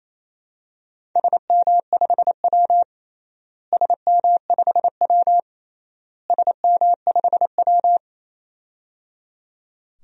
Morse code, heard as a keyed tone, about 28 wpm, 700 Hz. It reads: SM5W SM5W SM5W
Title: DXCC-100 Call Signs - 4 Characters - Repeated 3x - 28wpm